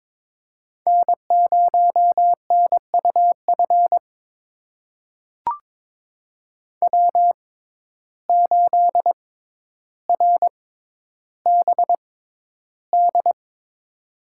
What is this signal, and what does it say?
Morse code, heard as a keyed tone, about 22 wpm, spaced as for 11 wpm, 700 Hz. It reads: N0NUF W8RBD
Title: USA Call Signs - 5 Characters - 6x Character Spacing - 22wpm